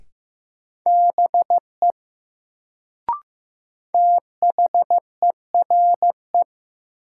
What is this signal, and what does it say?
Morse code, heard as a keyed tone, about 15 wpm, 700 Hz. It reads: BE THERE
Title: Sentences from Top 100 Words - New Word Drill - 15wpm